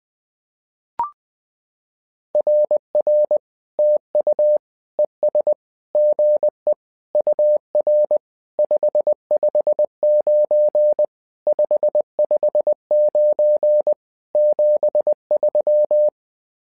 Morse code, heard as a keyed tone, about 20 wpm, 600 Hz. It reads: RR TU ES GE UR 559 559 73
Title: SOTA QSO Practice 20wpm